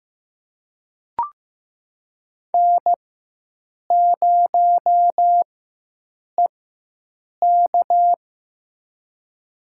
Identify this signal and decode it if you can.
Morse code, heard as a keyed tone, about 15 wpm, spaced as for 7 wpm, 700 Hz. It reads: N0EK
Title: USA Call Signs - 4 Characters - 4x Character Spacing - 15wpm